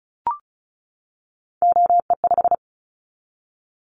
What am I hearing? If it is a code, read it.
Morse code, heard as a keyed tone, about 34 wpm, 700 Hz. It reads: OE5
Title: Callsign Prefixes for Top 100 DXCC - 35wpm